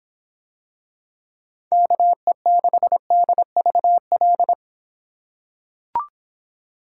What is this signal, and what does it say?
Morse code, heard as a keyed tone, about 24 wpm, 700 Hz. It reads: KE6DVL